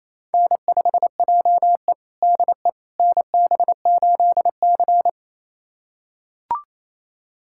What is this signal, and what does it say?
Morse code, heard as a keyed tone, about 28 wpm, 700 Hz. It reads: N5JE DE NB8C
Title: USA Callsign DE Callsign - 4 Characters - 28wpm